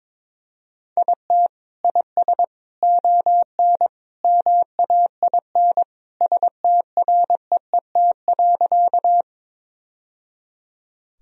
Morse code, heard as a keyed tone, about 22 wpm, 700 Hz. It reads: IT IS ON MAIN STREET.
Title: Sentences from Top 1000 Words - Review Part 2 - 22wpm